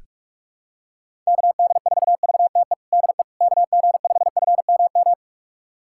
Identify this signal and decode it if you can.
Morse code, heard as a keyed tone, about 46 wpm, 700 Hz. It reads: KD4VTE DE KG5FMM
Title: USA Callsign DE Callsign - 6 Characters - 45wpm